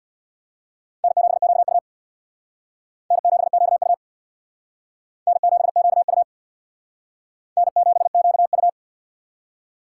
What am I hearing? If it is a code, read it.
Morse code, heard as a keyed tone, about 60 wpm, 690 Hz. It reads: N7QU N7QU N7QU N7QU
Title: Speed Racing - USA 1x2 Call Signs 50wpm